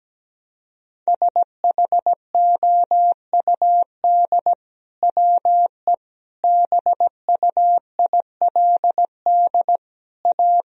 Morse code, heard as a keyed tone, about 17 wpm, 700 Hz. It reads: SHOUD WE BUILD A
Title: Sentences from Top 1000 Words - Review Part 4 - 17wpm